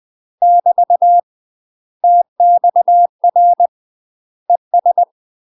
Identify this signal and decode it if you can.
Morse code, heard as a keyed tone, about 20 wpm, 700 Hz. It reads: <BT> TXR ES
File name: UK MASTER QSOs TRAINING / 20WPM / UK MASTER QSOs 0060